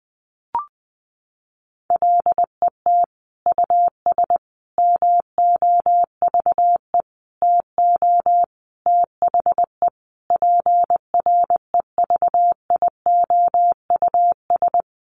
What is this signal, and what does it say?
Morse code, heard as a keyed tone, about 20 wpm, 700 Hz. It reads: LET US MOVE TO THE PREVIOUS
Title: Sentences from Top 1500 Words 20wpm